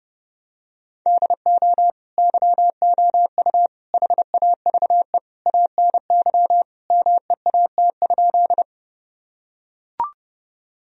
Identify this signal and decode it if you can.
Morse code, heard as a keyed tone, about 30 wpm, 700 Hz. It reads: DO YOU HAVE ANY MEAT?
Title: Sentences from Top 1000 Words - Review Part 3 - 30wpm